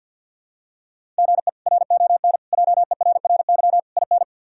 Morse code, heard as a keyed tone, about 49 wpm, 700 Hz. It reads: GE RON JERRY IN
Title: K1USN SST QSO Practice - 50wpm